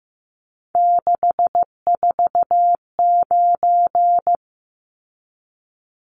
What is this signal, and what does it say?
Morse code, heard as a keyed tone, about 15 wpm, 700 Hz. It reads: 649
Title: Numbers - 3-Digit 15wpm